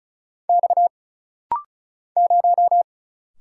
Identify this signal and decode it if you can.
Morse code, heard as a keyed tone, about 36 wpm, 700 Hz. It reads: X 0
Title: Single Letter-Number Alt - Rapid-Fire 35wpm